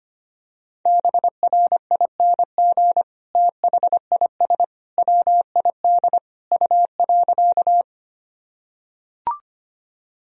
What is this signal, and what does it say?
Morse code, heard as a keyed tone, about 25 wpm, 700 Hz. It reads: BRING THIS WID U.
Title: Sentences from Top 500 Words and Abbreviations 25wpm